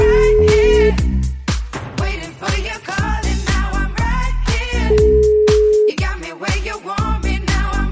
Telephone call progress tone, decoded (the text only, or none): ringback tone